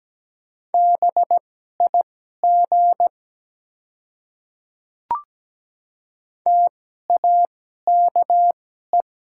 Morse code, heard as a keyed tone, about 17 wpm, spaced as for 12 wpm, 700 Hz. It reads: BIG TAKE
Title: 2x Character Spacing - Top 300 Words - 17wpm